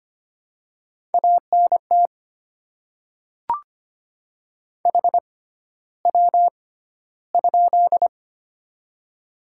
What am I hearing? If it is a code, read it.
Morse code, heard as a keyed tone, about 25 wpm, spaced as for 13 wpm, 700 Hz. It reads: ANT HW?
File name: Top 20 QSO Elements - 6x Character Spacing - 25wpm